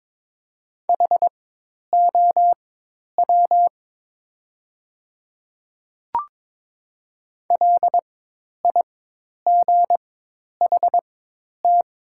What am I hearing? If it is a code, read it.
Morse code, heard as a keyed tone, about 22 wpm, spaced as for 10 wpm, 700 Hz. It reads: HOW LIGHT
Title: 4x Character Spacing - Top 300 Words - 22wpm